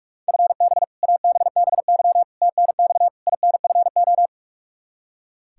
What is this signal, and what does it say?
Morse code, heard as a keyed tone, about 46 wpm, 700 Hz. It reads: FB ABBY TNX INFO
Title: Practice QSO - Exchange 3 - More Info - 45wpm